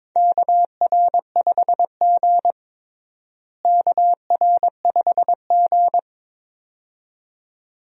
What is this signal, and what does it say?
Morse code, heard as a keyed tone, about 22 wpm, 700 Hz. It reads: KR5G KR5G
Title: USA Call Signs - 4 Characters - Repeated 2x - 22wpm